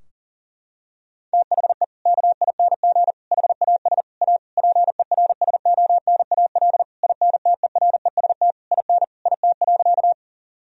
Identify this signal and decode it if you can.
Morse code, heard as a keyed tone, about 40 wpm, 690 Hz. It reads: THE KING HAS A PERSONAL INTEREST IN IT.
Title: Sentences from Top 1000 Words - Review Part 4 - 40wpm